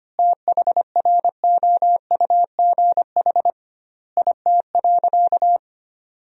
Morse code, heard as a keyed tone, about 25 wpm, 700 Hz. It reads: THROUGH IT.
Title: 2x Word-Spacing - Sentences from Top 200 Words - 25wpm